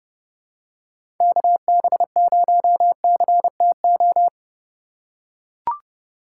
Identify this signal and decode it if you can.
Morse code, heard as a keyed tone, about 30 wpm, 700 Hz. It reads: KB0CTO
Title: USA Call Signs - 6 Characters - 4x Character Spacing - 30wpm